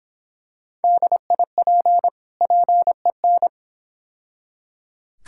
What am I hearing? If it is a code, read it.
Morse code, heard as a keyed tone, about 26 wpm, 700 Hz. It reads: DIP PEN